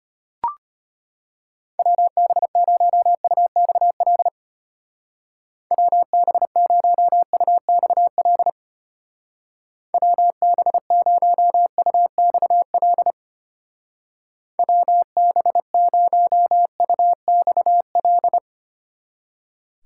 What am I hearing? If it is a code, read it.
Morse code, heard as a keyed tone, about 31 wpm, 700 Hz. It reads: WB0UXL WB0UXL WB0UXL WB0UXL
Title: Speed Racing - USA 2x3 Call Signs 25wpm